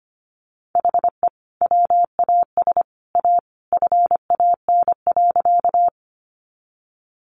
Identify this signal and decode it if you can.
Morse code, heard as a keyed tone, about 25 wpm, 700 Hz. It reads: HE WAS A FAN.
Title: Sentences from Top 1300 Words 25wpm